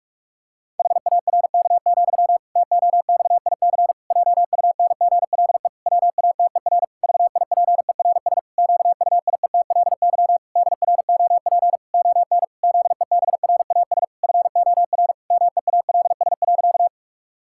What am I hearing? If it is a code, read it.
Morse code, heard as a keyed tone, about 46 wpm, 700 Hz. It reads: HARK, TOXIC JUNGLE WATER VIPERS QUIETLY DROP ON ZEBRAS FOR MEALS.
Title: Pangram Challenge v2 - 45wpm